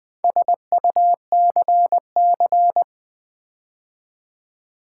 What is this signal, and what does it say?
Morse code, heard as a keyed tone, about 19 wpm, 700 Hz. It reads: SUCC
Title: Build A Word - 4 Letters At A Time - 12-Letter Words - 20wpm